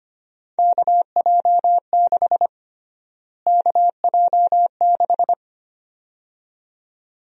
Morse code, heard as a keyed tone, about 25 wpm, 700 Hz. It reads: KJ6 KJ6
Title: USA Call Sign Prefixes - Repeated 2x - 25wpm